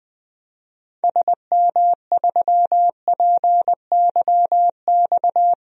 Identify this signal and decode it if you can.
Morse code, heard as a keyed tone, about 20 wpm, 700 Hz. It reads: SM3PYX
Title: DXCC-100 Call Signs - 6 Characters - 20wpm